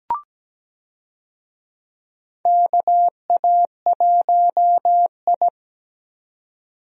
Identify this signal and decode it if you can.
Morse code, heard as a keyed tone, about 17 wpm, 700 Hz. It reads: KA1I
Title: USA Call Signs - 4 Characters - 17wpm